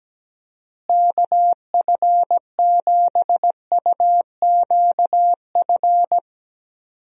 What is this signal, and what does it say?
Morse code, heard as a keyed tone, about 17 wpm, 700 Hz. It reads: KF7UQF
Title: USA Call Signs - 6 Characters - Repeated 2x - 17wpm